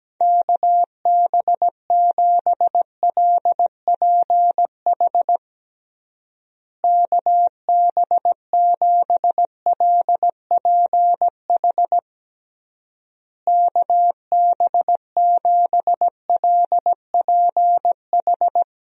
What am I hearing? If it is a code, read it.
Morse code, heard as a keyed tone, about 17 wpm, 700 Hz. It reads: KB7LPH KB7LPH KB7LPH
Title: USA Call Signs - 6 Characters - Repeated 3x - 17wpm